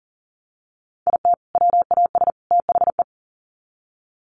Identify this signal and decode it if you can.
Morse code, heard as a keyed tone, about 40 wpm, 700 Hz. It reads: IT WAS THE
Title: Sets of 3 Words 40wpm